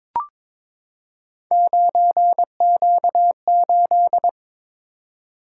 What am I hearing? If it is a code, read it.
Morse code, heard as a keyed tone, about 22 wpm, 700 Hz. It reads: 9Q8
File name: practice / 22wpm / Three Letter-Number Combinations - 22wpm